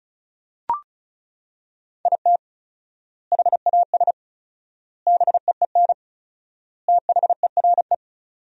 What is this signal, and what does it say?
Morse code, heard as a keyed tone, about 34 wpm, 700 Hz. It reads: IT HAS BEEN THERE
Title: 4x Word-Spacing - Sentences from Top 400 Words - 35wpm